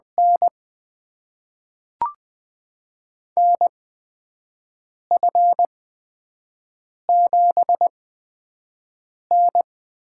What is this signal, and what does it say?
Morse code, heard as a keyed tone, about 20 wpm, spaced as for 5 wpm, 700 Hz. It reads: N NF7N
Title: USA Call Signs - 4 Characters - 8x Character Spacing - 20wpm